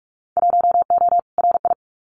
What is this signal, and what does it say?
Morse code, heard as a keyed tone, about 44 wpm, 700 Hz. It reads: 1O RI
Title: Winter Field Day Exchanges 45wpm